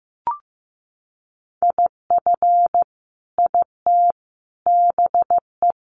Morse code, heard as a keyed tone, about 15 wpm, 700 Hz. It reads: IF IT BE
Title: Sentences from Top 1300 Words 15wpm